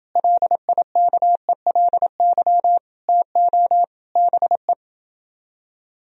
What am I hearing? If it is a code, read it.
Morse code, heard as a keyed tone, about 28 wpm, 700 Hz. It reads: LIKELY TO BE